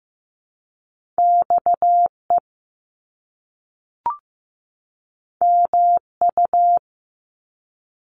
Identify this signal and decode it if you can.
Morse code, heard as a keyed tone, about 15 wpm, 700 Hz. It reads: XE MU
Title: Top Two Letter Combinations in Words 15wpm